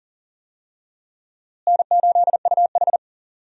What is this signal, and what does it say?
Morse code, heard as a keyed tone, about 40 wpm, 700 Hz. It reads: N8UH
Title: USA Call Signs - 4 Characters - Repeated 2x - 40wpm